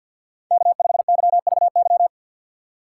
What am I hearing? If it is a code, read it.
Morse code, heard as a keyed tone, about 52 wpm, 700 Hz. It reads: K5YVY
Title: K1USN SST QSO Practice - 50wpm